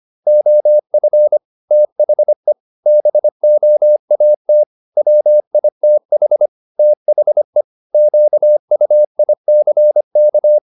Morse code, heard as a keyed tone, about 25 wpm, 600 Hz. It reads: OF THE BOAT WITH THE QUICK
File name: the old man and the sea part 1 25wpm